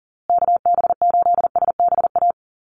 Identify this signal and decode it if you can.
Morse code, heard as a keyed tone, about 40 wpm, 700 Hz. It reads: KB8SBA
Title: USA Call Signs Repeated 3x 40wpm